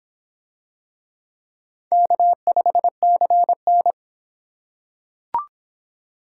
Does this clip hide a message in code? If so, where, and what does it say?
Yes: Morse code from 1.9 to 3.9 s, heard as a keyed tone, about 26 wpm, 700 Hz. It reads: K5CN